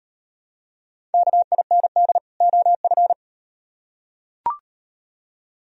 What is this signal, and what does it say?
Morse code, heard as a keyed tone, about 38 wpm, 700 Hz. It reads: KIND OF